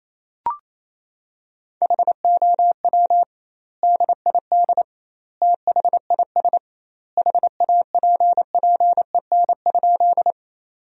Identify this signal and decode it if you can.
Morse code, heard as a keyed tone, about 28 wpm, 700 Hz. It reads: HOW DID THIS HAPPEN?
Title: 2x Word-Spacing - Sentences from Top 400 Words - 28wpm